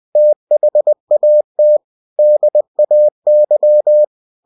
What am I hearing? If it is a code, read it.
Morse code, heard as a keyed tone, about 20 wpm, 600 Hz. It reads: THAT DAY
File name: cw-books / winnie-the-pooh / winnie-20wpm-page-0011